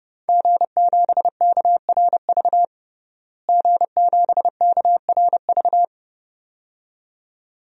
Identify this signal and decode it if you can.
Morse code, heard as a keyed tone, about 32 wpm, 700 Hz. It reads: G7KRV G7KRV
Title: DXCC-100 Call Signs - 5 Characters - Repeated 2x - 30wpm